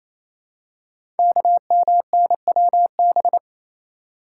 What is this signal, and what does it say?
Morse code, heard as a keyed tone, about 28 wpm, 700 Hz. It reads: KMNWB
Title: Code Groups - 5 Letters - 28wpm